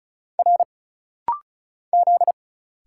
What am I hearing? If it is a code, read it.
Morse code, heard as a keyed tone, about 35 wpm, 700 Hz. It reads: R Z
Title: Single Letters - Rapid-Fire 35wpm